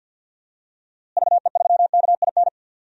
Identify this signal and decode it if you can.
Morse code, heard as a keyed tone, about 52 wpm, 700 Hz. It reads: VE3KIN
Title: DXCC-100 Call Signs - 6 Characters - Repeated 3x - 50wpm